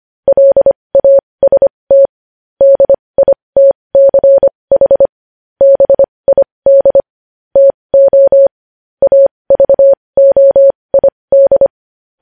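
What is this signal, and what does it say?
Morse code, heard as a keyed tone, about 25 wpm, spaced as for 20 wpm, 550 Hz. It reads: LAST DITCH BID TO AVOID
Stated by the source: Morse code news headlines sent at 20 words per minute.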